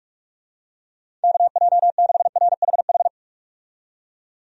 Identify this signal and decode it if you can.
Morse code, heard as a keyed tone, about 46 wpm, 700 Hz. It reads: KJ6RHH
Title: USA Call Signs - 6 Characters - 45wpm